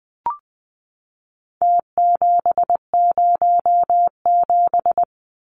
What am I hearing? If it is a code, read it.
Morse code, heard as a keyed tone, about 20 wpm, 700 Hz. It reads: T707
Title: Numbers - Contest Style - 4-Digit 20wpm